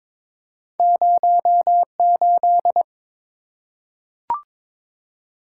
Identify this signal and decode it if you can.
Morse code, heard as a keyed tone, about 22 wpm, 690 Hz. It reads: 08
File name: practice / 22wpm / Two Letter-Number Combinations - 22wpm